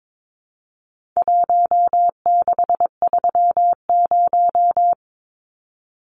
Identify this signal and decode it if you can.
Morse code, heard as a keyed tone, about 22 wpm, 700 Hz. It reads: 1630
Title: Numbers - Contest Style - 4-Digit 22wpm